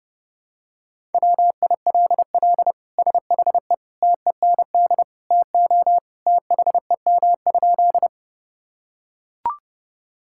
Morse code, heard as a keyed tone, about 32 wpm, 700 Hz. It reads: WILL SHE TEND TO THEM?
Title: Sentences from Top 1000 Words - Review Part 5 - 30wpm